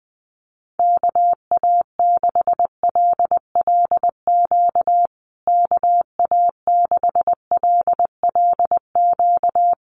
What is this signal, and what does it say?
Morse code, heard as a keyed tone, about 20 wpm, 700 Hz. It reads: KA6LLQ KA6LLQ
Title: USA Call Signs Repeated 3x 20wpm